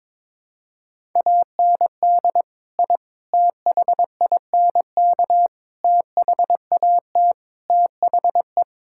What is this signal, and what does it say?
Morse code, heard as a keyed tone, about 21 wpm, 700 Hz. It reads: AND I THINK THAT THE
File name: practice / 22wpm / Build A Phrase - 5-Word Phrases - 22wpm